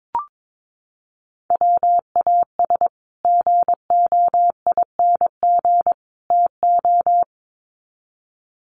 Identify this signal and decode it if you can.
Morse code, heard as a keyed tone, about 22 wpm, 700 Hz. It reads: WAS GOING TO
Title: Sets of 3 Words 22wpm